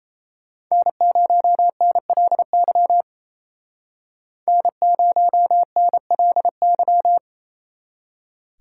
Morse code, heard as a keyed tone, about 34 wpm, 700 Hz. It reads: N0NLY N0NLY
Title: Speed Racing - USA 1x3 Call Signs 28wpm